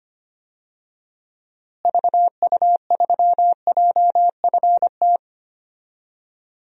Morse code, heard as a keyed tone, about 25 wpm, 700 Hz. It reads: VU3JFT
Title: DXCC-100 Call Signs - 6 Characters - Repeated 3x - 25wpm